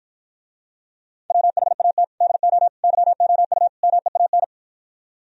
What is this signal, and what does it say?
Morse code, heard as a keyed tone, about 52 wpm, 700 Hz. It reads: WHAT DO YOU MEAN